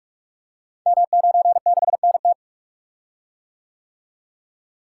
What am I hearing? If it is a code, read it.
Morse code, heard as a keyed tone, about 44 wpm, 700 Hz. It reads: M96NT
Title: Code Groups - 5 Letter:Numbers - 45wpm